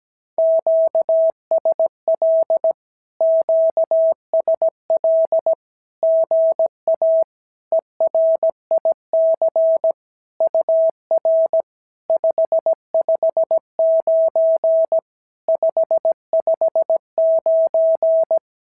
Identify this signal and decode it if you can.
Morse code, heard as a keyed tone, about 17 wpm, 650 Hz. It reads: QSL QSL GA ERIC UR 559 559
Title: SOTA QSO Practice 17wpm